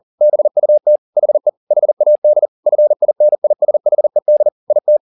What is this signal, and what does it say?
Morse code, heard as a keyed tone, about 40 wpm, 600 Hz. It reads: BUT HE HAD FINISHED IT
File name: the old man and the sea 2 40wpm